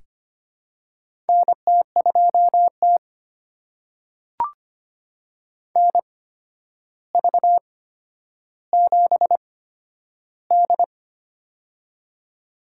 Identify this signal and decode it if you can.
Morse code, heard as a keyed tone, about 25 wpm, spaced as for 11 wpm, 700 Hz. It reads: NT2T NV7D
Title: USA Call Signs - 4 Characters - 8x Character Spacing - 25wpm